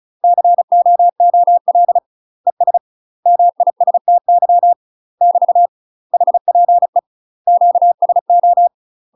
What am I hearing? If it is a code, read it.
Morse code, heard as a keyed tone, about 34 wpm, 700 Hz. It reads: COOL ES MISTY <BT> HPE QSO
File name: UK MASTER QSOs TRAINING / 35WPM / UK MASTER QSOs 0040